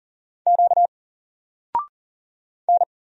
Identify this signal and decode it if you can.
Morse code, heard as a keyed tone, about 39 wpm, 700 Hz. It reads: Q N
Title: Single Letter-Number Alt 40wpm